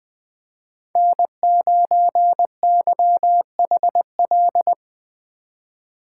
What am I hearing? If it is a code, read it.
Morse code, heard as a keyed tone, about 20 wpm, 700 Hz. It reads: N9YHL
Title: Speed Racing - USA Call Signs 17wpm